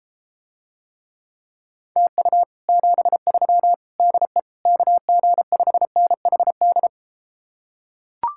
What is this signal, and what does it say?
Morse code, heard as a keyed tone, about 33 wpm, 700 Hz. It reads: TU 73 DE KG5NHD